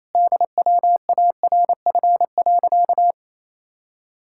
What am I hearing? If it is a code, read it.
Morse code, heard as a keyed tone, about 27 wpm, 700 Hz. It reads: DWARF.
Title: Pangram Challenge v2 - 28wpm